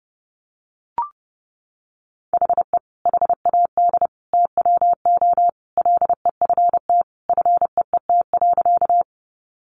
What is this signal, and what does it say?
Morse code, heard as a keyed tone, about 30 wpm, 700 Hz. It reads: HE HAD TWO LEFT FEET.
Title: Sentences from Top 300 Words 30wpm